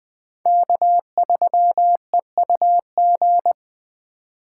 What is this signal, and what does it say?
Morse code, heard as a keyed tone, about 20 wpm, 700 Hz. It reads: K3EUG
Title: USA Call Signs - 5 Characters - 2x Character Spacing - 20wpm